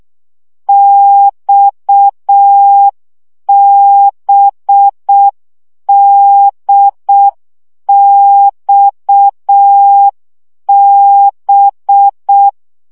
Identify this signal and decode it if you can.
Morse code, heard as a keyed tone, about 6 wpm, 800 Hz. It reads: XBDXB
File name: CW gyakorlas d b x 5 baud